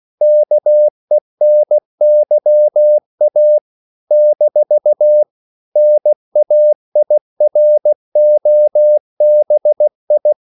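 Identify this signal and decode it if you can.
Morse code, heard as a keyed tone, about 16 wpm, 600 Hz. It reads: KENYA - NAIROBI
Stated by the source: CW practice of national capitals at 16 words per minute, from ON6ZQ.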